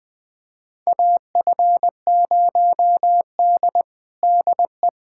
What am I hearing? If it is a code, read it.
Morse code, heard as a keyed tone, about 20 wpm, 700 Hz. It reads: AF0D DE
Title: USA Callsign DE Callsign - 4 Characters - 20wpm